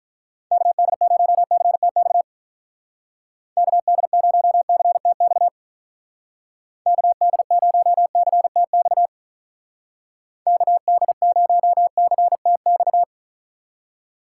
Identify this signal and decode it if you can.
Morse code, heard as a keyed tone, about 46 wpm, 700 Hz. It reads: KD0CTX KD0CTX KD0CTX KD0CTX